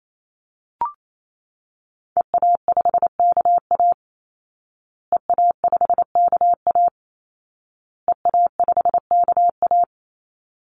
Morse code, heard as a keyed tone, about 28 wpm, 700 Hz. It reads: EA5KA EA5KA EA5KA
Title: Call Signs Repeated 2x - 28wpm